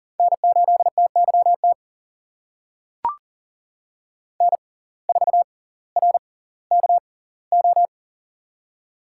Morse code, heard as a keyed tone, about 40 wpm, spaced as for 20 wpm, 700 Hz. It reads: N8TYT N4RKO
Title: USA Call Signs - 5 Characters - 6x Character Spacing - 40wpm